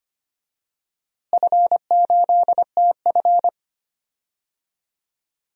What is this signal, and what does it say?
Morse code, heard as a keyed tone, about 25 wpm, 700 Hz. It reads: F8TF
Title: DXCC-100 Call Signs - 4 Characters - Repeated 2x - 25wpm